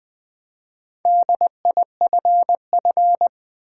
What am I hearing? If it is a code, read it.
Morse code, heard as a keyed tone, about 20 wpm, 700 Hz. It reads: DIFF
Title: Build A Word - 4 Letters At A Time - 10-Letter Words - 20wpm